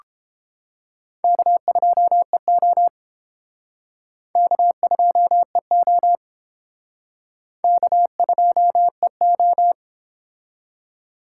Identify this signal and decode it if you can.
Morse code, heard as a keyed tone, about 30 wpm, 700 Hz. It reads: K2EO K2EO K2EO